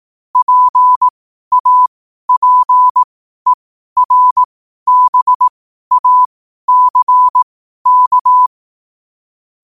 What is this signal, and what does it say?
Morse code, heard as a keyed tone, about 18 wpm, spaced as for 12 wpm, 1000 Hz. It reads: PAPERBACK
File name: qotdfiles / 12wpm